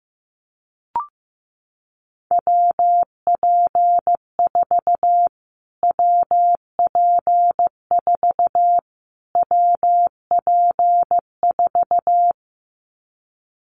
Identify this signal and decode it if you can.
Morse code, heard as a keyed tone, about 15 wpm, 700 Hz. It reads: WP4 WP4 WP4
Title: Callsign Prefixes for Top 25 DXCC - Repeated 3x - 15wpm